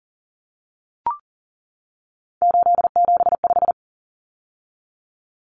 Morse code, heard as a keyed tone, about 40 wpm, 700 Hz. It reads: 875
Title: Numbers - 3-Digit 40wpm